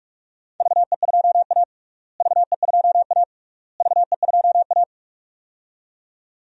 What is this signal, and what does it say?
Morse code, heard as a keyed tone, about 42 wpm, 700 Hz. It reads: VE2A VE2A VE2A
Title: DXCC-100 Call Signs - 4 Characters - Repeated 3x - 45wpm